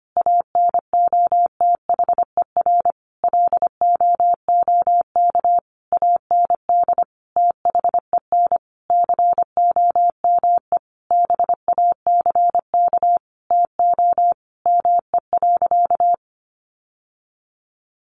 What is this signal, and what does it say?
Morse code, heard as a keyed tone, about 25 wpm, 700 Hz. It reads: ANOTHER LOOK AND THEN COME BACK TO ME.
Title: Sentences from Top 600 Words 25wpm